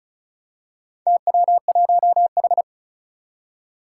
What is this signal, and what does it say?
Morse code, heard as a keyed tone, about 34 wpm, 700 Hz. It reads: TW1H
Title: Four Challenging Letter-Number Combinations 35wpm